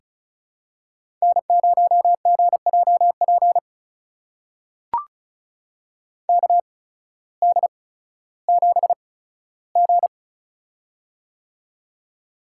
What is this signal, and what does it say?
Morse code, heard as a keyed tone, about 34 wpm, spaced as for 16 wpm, 700 Hz. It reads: N0GJP KD7G
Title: USA Call Signs - 8x Character Spacing - 35wpm